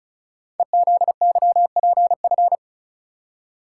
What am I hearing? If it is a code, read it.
Morse code, heard as a keyed tone, about 36 wpm, 690 Hz. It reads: EZYPF
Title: Code Groups - 5 Letters - 35wpm